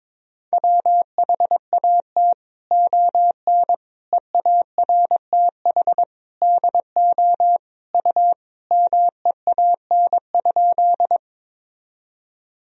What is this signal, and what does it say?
Morse code, heard as a keyed tone, about 22 wpm, 700 Hz. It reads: WHAT ON EARTH DO U MEAN?
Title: Sentences from Top 200 Words and Abbreviations 22wpm